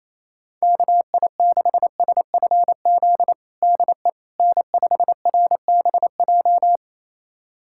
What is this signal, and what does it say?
Morse code, heard as a keyed tone, about 28 wpm, 700 Hz. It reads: KI6SFZ DE N5RBJ
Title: USA Callsign DE Callsign 28wpm